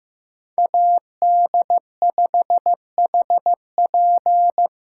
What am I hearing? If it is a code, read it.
Morse code, heard as a keyed tone, about 18 wpm, 700 Hz. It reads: AD5HP
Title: Speed Racing - USA 2x2 Call Signs 15wpm